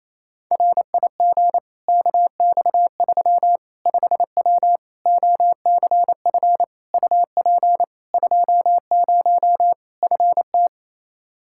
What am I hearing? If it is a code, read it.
Morse code, heard as a keyed tone, about 28 wpm, 700 Hz. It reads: RIG KX3 5W OCF UP 20 FT
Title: Practice QSO - Exchange 2 - Info - 28wpm